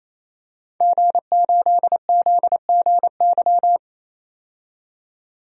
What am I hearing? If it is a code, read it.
Morse code, heard as a keyed tone, about 28 wpm, 700 Hz. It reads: G8ZGY
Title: DXCC-100 Call Signs - 5 Characters - 28wpm